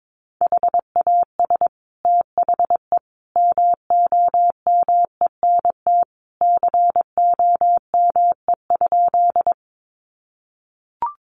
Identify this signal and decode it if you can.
Morse code, heard as a keyed tone, about 22 wpm, 700 Hz. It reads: HAS THE MOMENT COME?
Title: Sentences from Top 1000 Words 22wpm